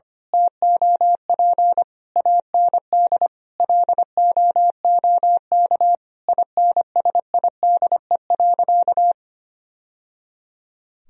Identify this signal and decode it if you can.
Morse code, heard as a keyed tone, about 25 wpm, 700 Hz. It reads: TOP AND LOOK INSIDE.
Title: Sentences from Top 1000 Words - Review Part 2 - 25wpm